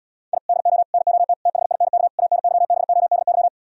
Morse code, heard as a keyed tone, about 77 wpm, 700 Hz. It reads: I DO NOT THINK ANYBODY